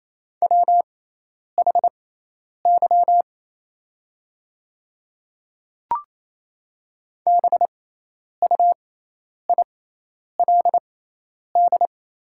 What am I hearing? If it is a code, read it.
Morse code, heard as a keyed tone, about 28 wpm, spaced as for 10 wpm, 700 Hz. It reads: WHY BUILD
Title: 6x Character Spacing - Top 300 Words - 28wpm